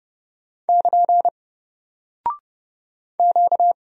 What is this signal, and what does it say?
Morse code, heard as a keyed tone, about 30 wpm, 700 Hz. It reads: <KN> Q
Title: Single Character (Letter:Number:Punctuation:Prosign) Alt 30wpm